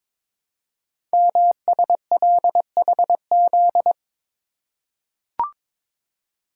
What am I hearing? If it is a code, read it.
Morse code, heard as a keyed tone, about 22 wpm, 700 Hz. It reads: MSLHZ
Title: Code Groups - 5 Letter:Numbers - 22wpm